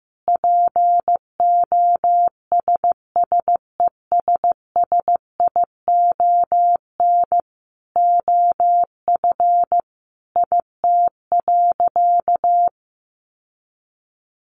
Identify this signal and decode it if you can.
Morse code, heard as a keyed tone, about 15 wpm, 700 Hz. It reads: POSSESSION OF IT.
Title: Sentences from Top 1400 Words 15wpm